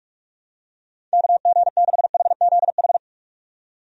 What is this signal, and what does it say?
Morse code, heard as a keyed tone, about 46 wpm, 700 Hz. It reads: KG6HZH
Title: USA Call Signs - 6 Characters - 45wpm